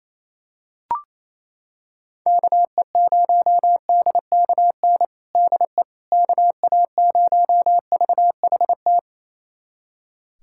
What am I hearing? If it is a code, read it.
Morse code, heard as a keyed tone, about 28 wpm, 700 Hz. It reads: KE0DKN DE KA0VHT
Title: USA Callsign DE Callsign - 6 Characters - 28wpm